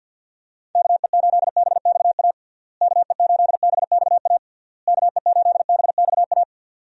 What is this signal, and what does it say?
Morse code, heard as a keyed tone, about 50 wpm, 700 Hz. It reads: KE8BXA KE8BXA KE8BXA
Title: USA Call Signs - 6 Characters - Repeated 3x - 50wpm